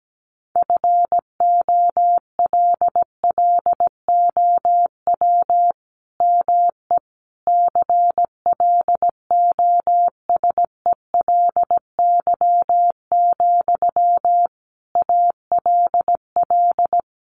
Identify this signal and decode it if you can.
Morse code, heard as a keyed tone, about 17 wpm, 700 Hz. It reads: FOLLOW ME CLOSELY, ALL
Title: Sentences from Top 1400 Words 17wpm